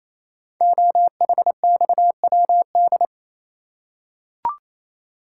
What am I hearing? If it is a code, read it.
Morse code, heard as a keyed tone, about 27 wpm, 700 Hz. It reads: OHXWD
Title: Code Groups - 5 Letters - 28wpm